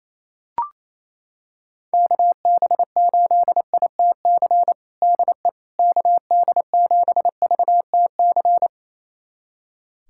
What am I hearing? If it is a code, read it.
Morse code, heard as a keyed tone, about 28 wpm, 700 Hz. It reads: KB8ITC DE KD7VTC
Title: USA Callsign DE Callsign - 6 Characters - 28wpm